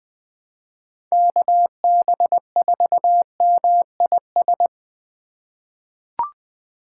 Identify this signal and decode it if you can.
Morse code, heard as a keyed tone, about 20 wpm, 700 Hz. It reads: KB4MIS
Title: USA Call Signs - 3x Character Spacing - 20wpm